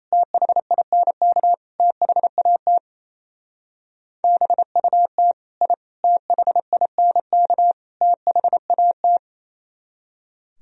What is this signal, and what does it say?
Morse code, heard as a keyed tone, about 34 wpm, 700 Hz. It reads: THINK THAT BUT I THINK THAT
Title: Speed-Racing - Sets of 4 Words 28wpm